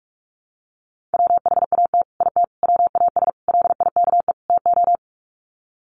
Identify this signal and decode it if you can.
Morse code, heard as a keyed tone, about 44 wpm, 700 Hz. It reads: WHAT IT WAS LIKE TO
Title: Sets of 5 Words Encore 45wpm